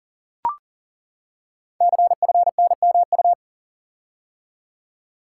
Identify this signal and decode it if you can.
Morse code, heard as a keyed tone, about 38 wpm, 690 Hz. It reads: CFNMU
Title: Code Groups - 5 Letters - 40wpm